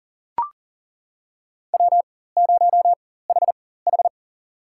Morse code, heard as a keyed tone, about 40 wpm, spaced as for 19 wpm, 700 Hz. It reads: W0HH
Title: USA Call Signs - 4 Characters - 4x Character Spacing - 40wpm